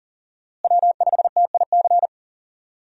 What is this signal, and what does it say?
Morse code, heard as a keyed tone, about 39 wpm, 700 Hz. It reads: W5TIC
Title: USA Call Signs - 5 Characters - Repeated 2x - 40wpm